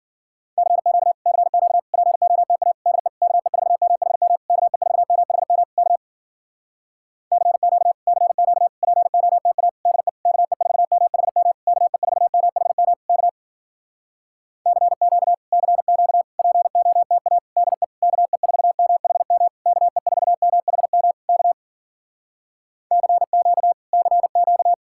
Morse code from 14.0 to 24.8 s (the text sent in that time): CQ CQ POTA DE KE4MHM KE4MHM K CQ CQ